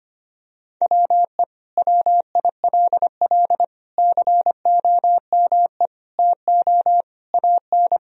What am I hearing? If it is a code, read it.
Morse code, heard as a keyed tone, about 25 wpm, 700 Hz. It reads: WE WILL COME TO AN
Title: Sentences from Top 1000 Words - Review Part 3 - 25wpm